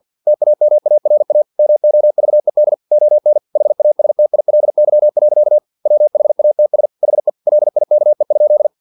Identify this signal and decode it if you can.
Morse code, heard as a keyed tone, about 50 wpm, 600 Hz. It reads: TAMARA MOVED ON HASTILY. WHATS HE LIKE?
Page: Never 49wpm